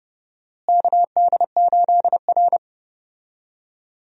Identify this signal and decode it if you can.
Morse code, heard as a keyed tone, about 30 wpm, 700 Hz. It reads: KD8R
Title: USA Call Signs - 4 Characters - Repeated 3x - 30wpm